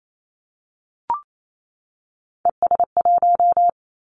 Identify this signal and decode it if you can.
Morse code, heard as a keyed tone, about 28 wpm, 700 Hz. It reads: ES1
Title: Callsign Prefixes for Top 50 DXCC - 28wpm